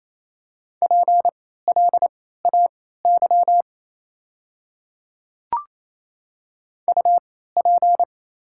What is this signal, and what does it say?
Morse code, heard as a keyed tone, about 28 wpm, spaced as for 16 wpm, 700 Hz. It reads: PLAY UP
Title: 3x Character Spacing - Top 200 Words - 28wpm